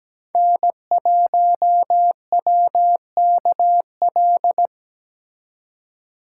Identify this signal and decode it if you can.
Morse code, heard as a keyed tone, about 20 wpm, 700 Hz. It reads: N1WKL
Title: Speed Racing - USA 1x3 Call Signs 17wpm